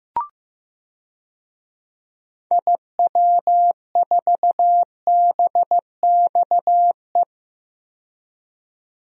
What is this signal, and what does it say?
Morse code, heard as a keyed tone, about 15 wpm, 700 Hz. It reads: IW4BXE
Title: DXCC-100 Call Signs - 6 Characters - 15wpm